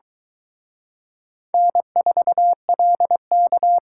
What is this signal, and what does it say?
Morse code, heard as a keyed tone, about 23 wpm, 700 Hz. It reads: N4LK